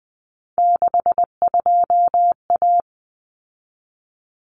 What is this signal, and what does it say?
Morse code, heard as a keyed tone, about 20 wpm, 700 Hz. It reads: 62A
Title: Common Cut Numbers - 3-Digit 20wpm